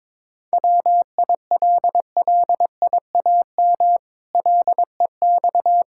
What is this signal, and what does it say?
Morse code, heard as a keyed tone, about 22 wpm, 700 Hz. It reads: WILLIAM LEX
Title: Pangram Challenge v2 - 22wpm